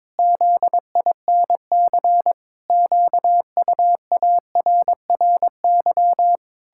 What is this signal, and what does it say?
Morse code, heard as a keyed tone, about 22 wpm, 700 Hz. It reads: ZINC QUARRY
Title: Pangram Challenge v2 - 22wpm